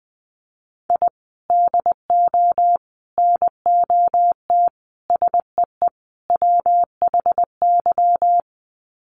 Morse code, heard as a keyed tone, about 20 wpm, 700 Hz. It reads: I DO NOT SEE WHY
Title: Sets of 5 Words Encore 20wpm